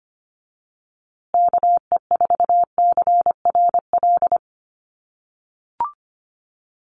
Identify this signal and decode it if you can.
Morse code, heard as a keyed tone, about 25 wpm, 700 Hz. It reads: KE4CRL
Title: USA Call Signs Repeated 3x 25wpm